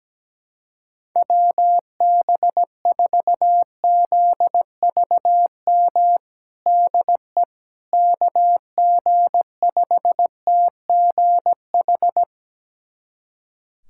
Morse code, heard as a keyed tone, about 17 wpm, 700 Hz. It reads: WB4ZVM DE KG5TGH
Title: USA Callsign DE Callsign - 6 Characters - 17wpm